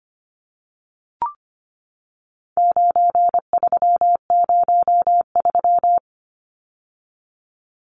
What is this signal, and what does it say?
Morse code, heard as a keyed tone, about 25 wpm, 700 Hz. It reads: 9303
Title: Numbers - 4-Digit 25wpm